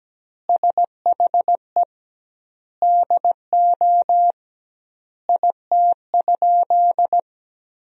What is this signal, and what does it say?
Morse code, heard as a keyed tone, about 17 wpm, 700 Hz. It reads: SHE DO IT?
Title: 2x Word-Spacing - Sentences from Top 100 Words - 17wpm